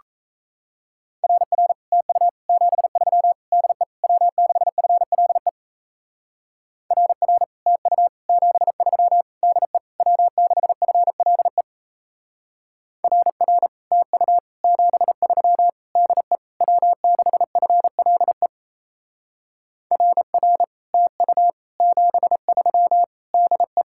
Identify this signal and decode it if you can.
Morse code, heard as a keyed tone, about 38 wpm, 700 Hz. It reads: RR TU 73 DE W6FLE RR TU 73 DE W6FLE RR TU 73 DE W6FLE RR TU 73 DE